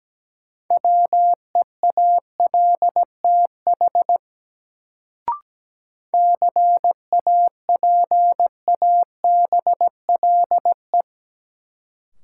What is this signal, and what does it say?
Morse code, heard as a keyed tone, about 17 wpm, 700 Hz. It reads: WEALTH CAPABLE
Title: Sentences from Top 1200 Words - New Word Drill Alt - 17wpm